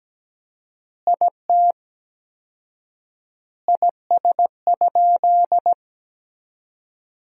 Morse code, heard as a keyed tone, about 17 wpm, 700 Hz. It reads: IT IS?
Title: 4x Word-Spacing - Sentences from Top 100 Words - 17wpm